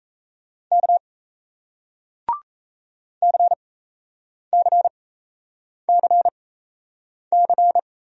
Morse code, heard as a keyed tone, about 37 wpm, spaced as for 8 wpm, 700 Hz. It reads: K CCCC